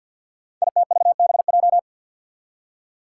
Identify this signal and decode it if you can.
Morse code, heard as a keyed tone, about 47 wpm, 700 Hz. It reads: ITVBJ
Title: Code Groups - 5 Letters - 50wpm